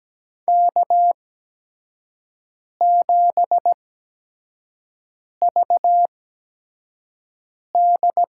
Morse code, heard as a keyed tone, about 17 wpm, spaced as for 5 wpm, 700 Hz. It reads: K7VD
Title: USA Call Signs - 8x Character Spacing - 17wpm